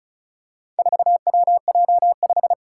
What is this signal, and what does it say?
Morse code, heard as a keyed tone, about 36 wpm, 700 Hz. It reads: 4WJ5
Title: Four Challenging Letter-Number Combinations 35wpm